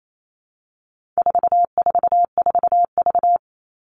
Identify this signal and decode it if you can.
Morse code, heard as a keyed tone, about 28 wpm, 700 Hz. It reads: 444V
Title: All Cut Numbers - 4-Digit 28wpm